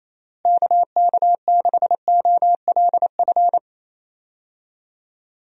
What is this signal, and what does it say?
Morse code, heard as a keyed tone, about 28 wpm, 700 Hz. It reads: KK6OLF
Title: USA Call Signs - 6 Characters - Repeated 3x - 28wpm